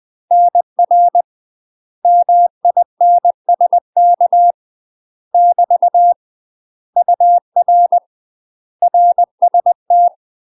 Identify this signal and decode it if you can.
Morse code, heard as a keyed tone, about 20 wpm, 700 Hz. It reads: NR MINSK <BT> UR RST
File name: UK MASTER QSOs TRAINING / 20WPM / UK MASTER QSOs 0024